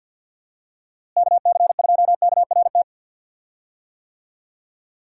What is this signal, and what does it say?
Morse code, heard as a keyed tone, about 49 wpm, 700 Hz. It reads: KC2KRT
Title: USA Call Signs - 6 Characters - Repeated 2x - 50wpm